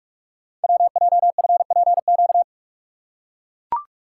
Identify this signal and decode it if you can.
Morse code, heard as a keyed tone, about 44 wpm, 700 Hz. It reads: WJFPQ
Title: Code Groups - 5 Letters - 45wpm